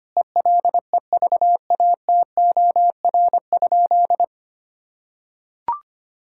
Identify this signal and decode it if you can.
Morse code, heard as a keyed tone, about 25 wpm, 700 Hz. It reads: ELEVATOR?
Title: Sentences from Top 1600 Words 25wpm